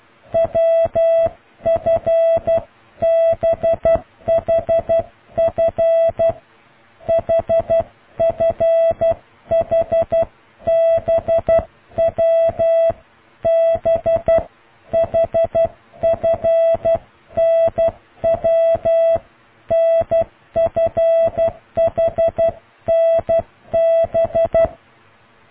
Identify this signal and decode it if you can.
Morse code, heard as a keyed tone, about 12 wpm, 660 Hz. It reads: WFBHF HFHBW BHFNW NFHNB